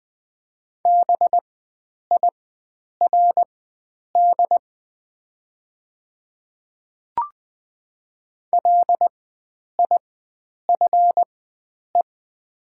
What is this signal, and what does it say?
Morse code, heard as a keyed tone, about 20 wpm, spaced as for 9 wpm, 700 Hz. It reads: BIRD LIFE
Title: 4x Character Spacing - Top 500 Words - 20wpm